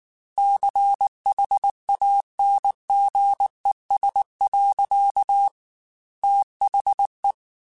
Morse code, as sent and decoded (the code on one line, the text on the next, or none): -.-. .... .- -. --. ... .-.-.- / - .... .
CHANGES. THE